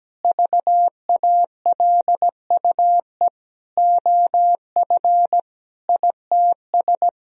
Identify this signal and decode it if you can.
Morse code, heard as a keyed tone, about 17 wpm, 700 Hz. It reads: VALUE OF ITS
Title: Sentences from Top 1000 Words - Review Part 4 - 17wpm